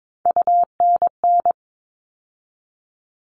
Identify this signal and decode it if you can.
Morse code, heard as a keyed tone, about 22 wpm, 700 Hz. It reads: UNN
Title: All Cut Numbers - 3-Digit 22wpm